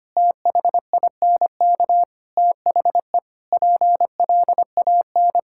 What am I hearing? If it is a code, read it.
Morse code, heard as a keyed tone, about 25 wpm, 700 Hz. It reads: THINK THE PLAN
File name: practice / 25wpm / Sentences with 4-Word Phrases - 25wpm